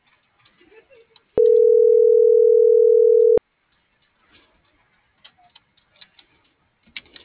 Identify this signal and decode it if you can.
Ringback tone (North America), 440 + 480 Hz.